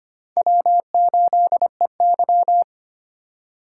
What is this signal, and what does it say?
Morse code, heard as a keyed tone, about 25 wpm, 700 Hz. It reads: W8EY
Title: USA Call Signs - 4 Characters - 2x Character Spacing - 25wpm